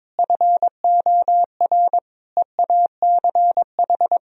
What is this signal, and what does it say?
Morse code, heard as a keyed tone, about 21 wpm, 700 Hz. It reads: FOR EACH
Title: Sentences from Top 1000 Words - Review Part 4 - 22wpm